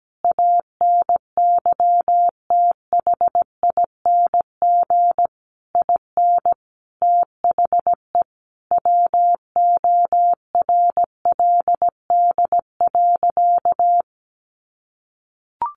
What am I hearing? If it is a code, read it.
Morse code, heard as a keyed tone, about 17 wpm, 700 Hz. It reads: ANYTHING IN THE WORLD.
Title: Sentences from Top 1000 Words 17wpm